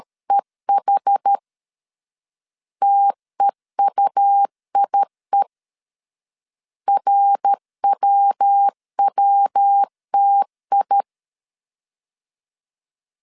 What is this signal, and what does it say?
Morse code, heard as a keyed tone, about 13 wpm, 800 Hz. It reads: EH TEUIE RWWTI
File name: CW / DET / 1 3 1500C S C 12wpm 2750